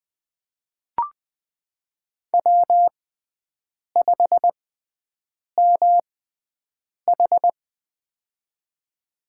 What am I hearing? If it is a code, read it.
Morse code, heard as a keyed tone, about 20 wpm, spaced as for 7 wpm, 700 Hz. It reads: W5MH
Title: USA Call Signs - 4 Characters - 6x Character Spacing - 20wpm